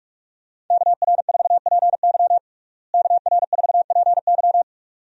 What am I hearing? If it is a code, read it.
Morse code, heard as a keyed tone, about 45 wpm, 700 Hz. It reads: KR4PY KR4PY
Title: USA Call Signs - 5 Characters - Repeated 2x - 45wpm